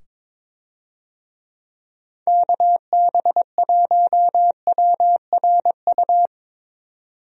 Morse code, heard as a keyed tone, about 22 wpm, 700 Hz. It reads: KB1WRU
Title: USA Call Signs - 6 Characters - 22wpm